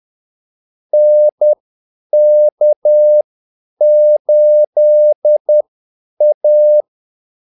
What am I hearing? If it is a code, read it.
Morse code, heard as a keyed tone, about 10 wpm, spaced as for 8 wpm, 600 Hz. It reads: NK8A
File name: Callsigns-10-8wpm